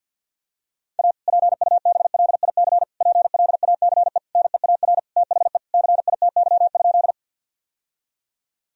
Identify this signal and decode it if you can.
Morse code, heard as a keyed tone, about 50 wpm, 700 Hz. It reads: A PUBLIC PLACE NEAR THE CITY?
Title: Sentences from Top 1000 Words - Review Part 3 - 50wpm